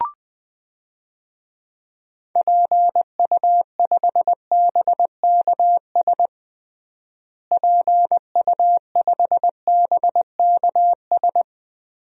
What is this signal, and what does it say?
Morse code, heard as a keyed tone, about 20 wpm, 700 Hz. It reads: PU5BKS PU5BKS
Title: DXCC-100 Call Signs - 6 Characters - Repeated 3x - 20wpm